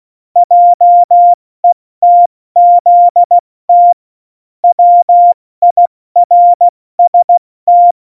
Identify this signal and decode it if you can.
Morse code, heard as a keyed tone, about 16 wpm, 700 Hz. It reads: JETZT WIRST